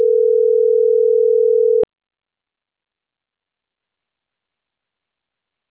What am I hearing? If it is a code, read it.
Ringback tone (North America), 440 + 480 Hz.